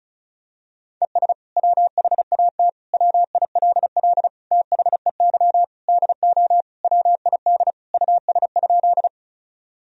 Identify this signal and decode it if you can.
Morse code, heard as a keyed tone, about 36 wpm, 700 Hz. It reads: ES WHAT WILL THEY DO WID US?
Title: Sentences from Top 200 Words and Abbreviations 35wpm